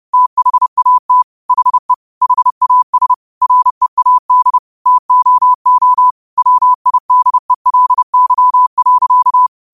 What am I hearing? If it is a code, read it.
Morse code, heard as a keyed tone, about 30 wpm, 1000 Hz. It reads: THAT HE HAS READ TOO WIDELY.
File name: qotdfiles / 30wpm